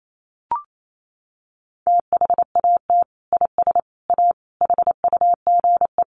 Morse code, heard as a keyed tone, about 28 wpm, 700 Hz. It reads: THAT IS A HUGE
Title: Sentences from Top 1100 Words 28wpm